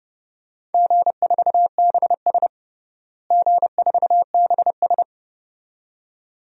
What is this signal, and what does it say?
Morse code, heard as a keyed tone, about 30 wpm, 700 Hz. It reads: G4BS G4BS
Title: DXCC-100 Call Signs - 4 Characters - Repeated 2x - 30wpm